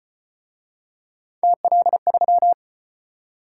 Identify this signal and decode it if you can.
Morse code, heard as a keyed tone, about 34 wpm, 700 Hz. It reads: TL3